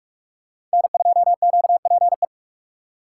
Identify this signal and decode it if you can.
Morse code, heard as a keyed tone, about 44 wpm, 700 Hz. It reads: N2QPE